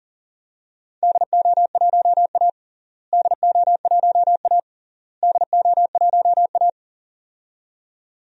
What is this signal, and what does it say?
Morse code, heard as a keyed tone, about 40 wpm, 700 Hz. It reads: DO1A DO1A DO1A
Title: DXCC-100 Call Signs - 4 Characters - Repeated 3x - 40wpm